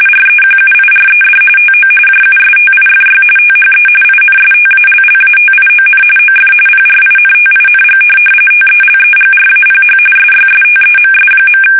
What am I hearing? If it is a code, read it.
RTTY, about 45.45 baud. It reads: REQUIRED TO JUSTIFY THE INSTALLATION OF PRINTING TELEGRAPH SYSTEMS.
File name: fsk850